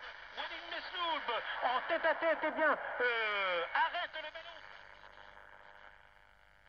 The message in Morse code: none